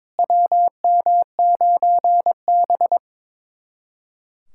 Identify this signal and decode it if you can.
Morse code, heard as a keyed tone, about 22 wpm, 700 Hz. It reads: WM9B
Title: USA Callsign DE Callsign - 4 Characters - 22wpm